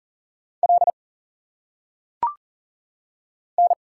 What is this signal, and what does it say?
Morse code, heard as a keyed tone, about 40 wpm, 700 Hz. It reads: L N
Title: Single Letters - Phonetic Pronunciation 40wpm